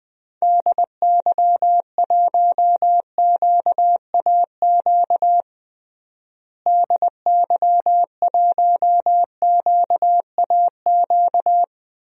Morse code, heard as a keyed tone, about 20 wpm, 700 Hz. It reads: DY1QAQ DY1QAQ
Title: DXCC-100 Call Signs - 6 Characters - Repeated 2x - 20wpm